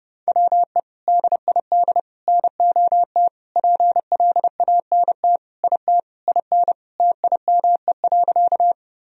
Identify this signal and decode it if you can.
Morse code, heard as a keyed tone, about 30 wpm, 700 Hz. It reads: WE DID NOT PLANT IT IN TIME.
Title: Sentences from Top 1000 Words - Review Part 1 - 30wpm